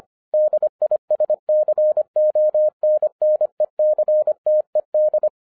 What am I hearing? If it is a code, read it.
Morse code, heard as a keyed tone, about 25 wpm, 610 Hz. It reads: DISCONNECTED
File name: 2012-01-mp3 / n14-FUN WITH NUMBERS-25wpm